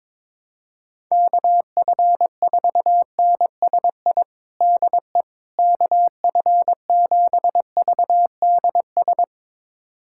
Morse code, heard as a keyed tone, about 22 wpm, 700 Hz. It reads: KF4NSI DE KF7VDS
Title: USA Callsign DE Callsign - 6 Characters - 22wpm